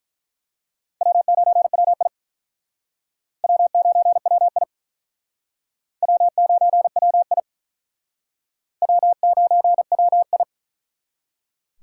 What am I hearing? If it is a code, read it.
Morse code, heard as a keyed tone, about 44 wpm, 700 Hz. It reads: W9WI W9WI W9WI W9WI
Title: Speed Racing - USA Call Signs 35wpm